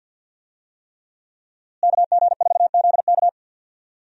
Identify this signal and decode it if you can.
Morse code, heard as a keyed tone, about 50 wpm, 700 Hz. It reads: KG4ZK
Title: USA Call Signs - 5 Characters - Repeated 3x - 50wpm